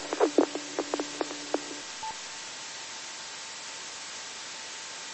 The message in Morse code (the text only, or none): none